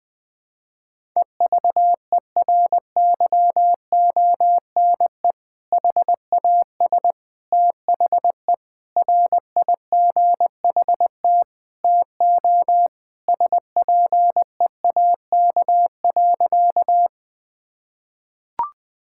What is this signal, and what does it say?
Morse code, heard as a keyed tone, about 20 wpm, 700 Hz. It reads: EVERYONE HAS THE RIGHT TO SPEAK.
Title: Sentences with 3-Word Phrases - 20wpm